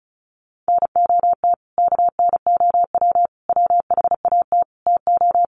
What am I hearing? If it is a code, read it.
Morse code, heard as a keyed tone, about 34 wpm, 700 Hz. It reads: NOT KNOW WHAT TO
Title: Sets of 5 Words 35wpm